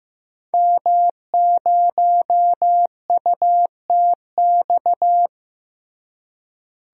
Morse code, heard as a keyed tone, about 16 wpm, 700 Hz. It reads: M0UTX
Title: DXCC-100 Call Signs - 5 Characters - Repeated 2x - 15wpm